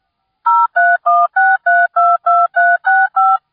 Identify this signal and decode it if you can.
DTMF, heard as telephone touch tones, tones of about 220 ms. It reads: 0316322365